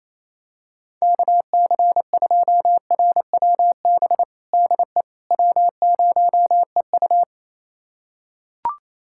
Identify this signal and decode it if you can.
Morse code, heard as a keyed tone, about 27 wpm, 700 Hz. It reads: KC2RWB DE W0EU
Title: USA Callsign DE Callsign 28wpm